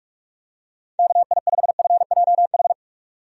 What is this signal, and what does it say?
Morse code, heard as a keyed tone, about 46 wpm, 700 Hz. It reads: KI5FJH
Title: USA Call Signs - 6 Characters - 45wpm